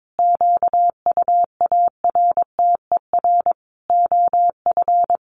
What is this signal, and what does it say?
Morse code, heard as a keyed tone, about 22 wpm, 700 Hz. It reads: QUARTER OF
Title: Sentences from Top 1300 Words 22wpm